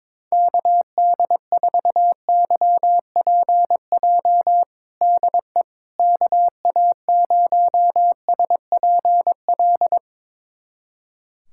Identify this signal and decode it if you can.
Morse code, heard as a keyed tone, about 22 wpm, 700 Hz. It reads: KD4YPJ DE KA0SPL
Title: USA Callsign DE Callsign - 6 Characters - 22wpm